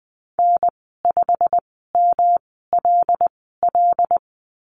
Morse code, heard as a keyed tone, about 20 wpm, spaced as for 15 wpm, 700 Hz. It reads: N5MLL
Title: 122 Lesson 20g - Number 3 - Call Signs 20wpm